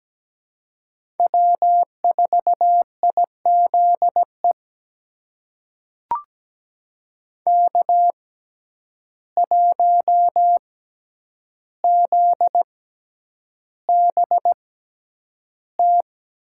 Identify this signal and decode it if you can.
Morse code, heard as a keyed tone, about 17 wpm, spaced as for 9 wpm, 700 Hz. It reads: W4IZE K1ZBT
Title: USA Call Signs - 5 Characters - 6x Character Spacing - 17wpm